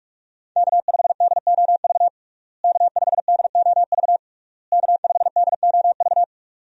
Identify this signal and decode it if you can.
Morse code, heard as a keyed tone, about 48 wpm, 700 Hz. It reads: K5DOV K5DOV K5DOV
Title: USA Call Signs - 5 Characters - Repeated 3x - 45wpm